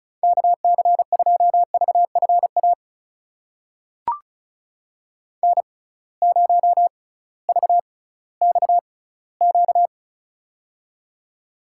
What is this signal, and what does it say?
Morse code, heard as a keyed tone, about 36 wpm, spaced as for 19 wpm, 700 Hz. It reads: KC2VFA N0VXQ
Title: USA Call Signs - 6x Character Spacing - 35wpm